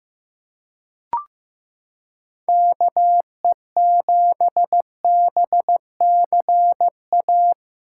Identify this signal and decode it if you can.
Morse code, heard as a keyed tone, about 16 wpm, 700 Hz. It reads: KE7BCA
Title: USA Callsign DE Callsign 15wpm